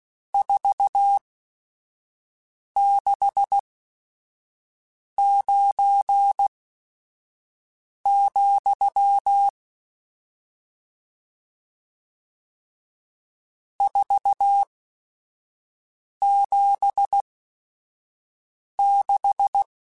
Morse code, heard as a keyed tone, about 16 wpm, spaced as for 5 wpm, 800 Hz. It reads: 469, 476
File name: RSTs / RST-mp3s / Various RST @5